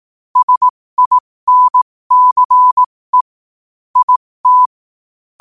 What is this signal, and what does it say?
Morse code, heard as a keyed tone, about 18 wpm, spaced as for 15 wpm, 1000 Hz. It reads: SINCE IT